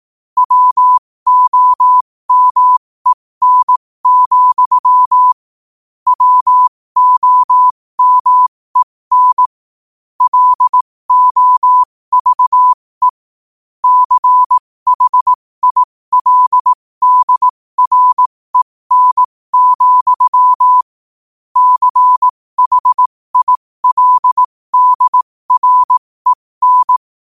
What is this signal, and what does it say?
Morse code, heard as a keyed tone, about 18 wpm, spaced as for 15 wpm, 1000 Hz. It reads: WOMEN, WOMEN LOVE CHILDREN, CHILDREN
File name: qotdfiles / 15wpm